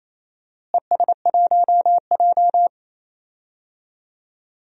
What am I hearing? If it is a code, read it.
Morse code, heard as a keyed tone, about 28 wpm, 700 Hz. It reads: ES1J
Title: DXCC-100 Call Signs - 4 Characters - Repeated 2x - 28wpm